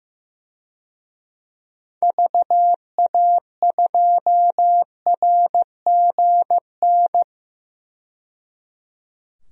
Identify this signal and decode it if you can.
Morse code, heard as a keyed tone, about 15 wpm, 700 Hz. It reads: VA2RGN
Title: DXCC-100 Call Signs - 6 Characters - 15wpm